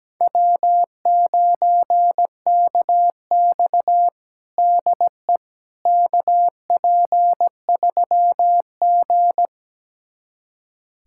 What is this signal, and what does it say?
Morse code, heard as a keyed tone, about 17 wpm, 700 Hz. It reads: W9KX DE KP3G
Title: USA Callsign DE Callsign - 4 Characters - 17wpm